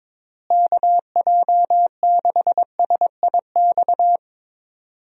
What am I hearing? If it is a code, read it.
Morse code, heard as a keyed tone, about 22 wpm, 700 Hz. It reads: KJ6SIX
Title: USA Call Signs - 6 Characters - Repeated 2x - 22wpm